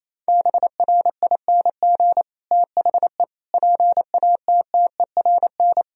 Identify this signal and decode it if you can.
Morse code, heard as a keyed tone, about 28 wpm, 700 Hz. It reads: BRING THE PATTERN
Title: Sentences from Top 1000 Words - Review Part 2 - 28wpm